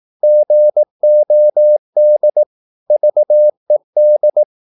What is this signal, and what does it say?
Morse code, heard as a keyed tone, about 18 wpm, 600 Hz. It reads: GOD VED
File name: Korte ord DK 18wpm